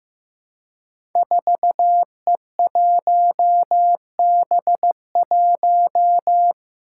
Morse code, heard as a keyed tone, about 15 wpm, 700 Hz. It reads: 4E1B1
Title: Five Challenging Letter-Number Combinations 15wpm